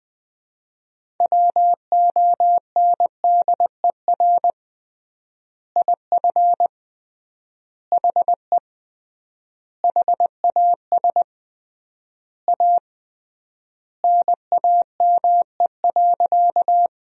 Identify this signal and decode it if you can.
Morse code, heard as a keyed tone, about 21 wpm, 700 Hz. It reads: WONDER IF HE HAS A NAME.
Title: 3x Word-Spacing - Sentences from Top 500 Words - 20wpm